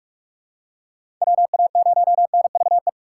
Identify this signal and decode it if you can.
Morse code, heard as a keyed tone, about 42 wpm, 700 Hz. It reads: WA0NVE
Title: USA Call Signs - 6 Characters - 4x Character Spacing - 45wpm